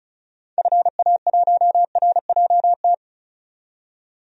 Morse code, heard as a keyed tone, about 36 wpm, 700 Hz. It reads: FA1RJT
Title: DXCC-100 Call Signs - 6 Characters - 35wpm